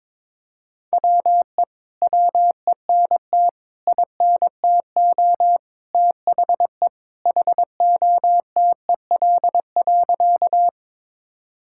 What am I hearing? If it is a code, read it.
Morse code, heard as a keyed tone, about 22 wpm, 700 Hz. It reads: WE WENT INTO THE HOTEL.
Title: Sentences from Top 1600 Words 22wpm